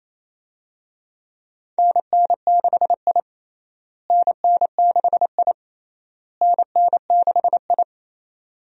Morse code, heard as a keyed tone, about 28 wpm, 700 Hz. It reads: NN6I NN6I NN6I
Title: USA Call Signs - 4 Characters - Repeated 3x - 28wpm